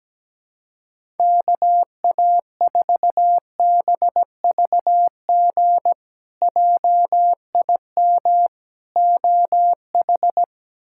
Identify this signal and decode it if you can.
Morse code, heard as a keyed tone, about 17 wpm, 700 Hz. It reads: KA4BVG JIM OH
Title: K1USN SST QSO Practice - 17wpm